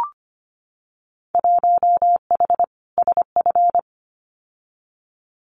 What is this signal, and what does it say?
Morse code, heard as a keyed tone, about 25 wpm, 700 Hz. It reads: 1H SF
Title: Winter Field Day Exchanges 25wpm